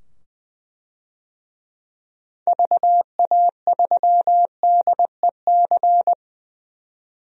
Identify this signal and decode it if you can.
Morse code, heard as a keyed tone, about 20 wpm, 700 Hz. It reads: VA3DEC
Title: DXCC-100 Call Signs - 6 Characters - 20wpm